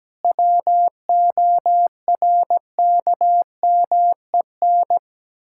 Morse code, heard as a keyed tone, about 17 wpm, 700 Hz. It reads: WORKMEN
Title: Pangram Challenge v2 - 17wpm